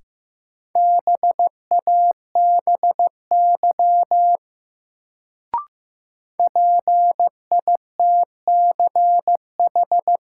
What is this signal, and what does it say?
Morse code, heard as a keyed tone, about 15 wpm, 700 Hz. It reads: BABY PITCH
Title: Sentences from Top 1300 Words - New Word Drill - 15wpm